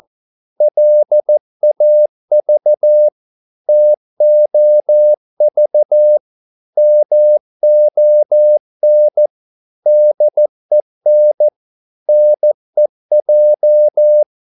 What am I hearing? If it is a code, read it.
Morse code, heard as a keyed tone, about 14 wpm, 600 Hz. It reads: LAV TOV MON DEN NEJ
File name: Korte ord DK 14wpm